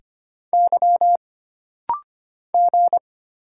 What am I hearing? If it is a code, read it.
Morse code, heard as a keyed tone, about 25 wpm, 700 Hz. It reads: Y G
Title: Single Letters - Rapid-Fire 25wpm